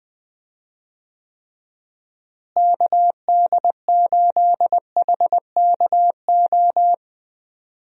Morse code, heard as a keyed tone, about 21 wpm, 700 Hz. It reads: KD8HKO
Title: USA Call Signs - 6 Characters - Repeated 3x - 20wpm